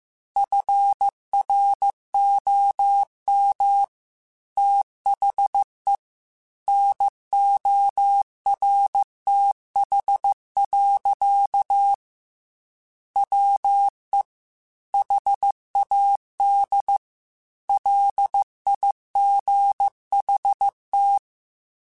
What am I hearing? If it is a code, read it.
Morse code, heard as a keyed tone, about 15 wpm, 800 Hz. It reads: FROM THE NORTH. WE HAD LIGHT